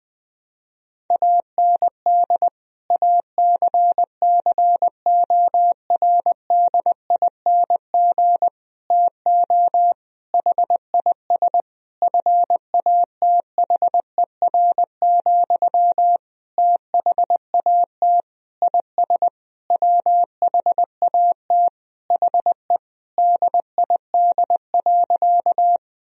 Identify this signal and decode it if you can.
Morse code, heard as a keyed tone, about 20 wpm, 700 Hz. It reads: AND ACCORDING TO HIS FATHER, THAT IS WHAT HE DID.
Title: Sentences from Top 1000 Words - Review Part 4 - 20wpm